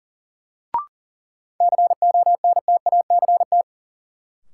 Morse code, heard as a keyed tone, about 40 wpm, 700 Hz. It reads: CONTACT
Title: Sentences from Top 1000 Words - Review Part 7 - New Word Drill - 40wpm